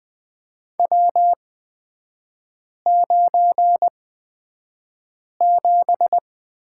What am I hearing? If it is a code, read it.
Morse code, heard as a keyed tone, about 20 wpm, spaced as for 5 wpm, 700 Hz. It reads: W97
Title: 224 Lesson 36e - BT - Three Characters 20wpm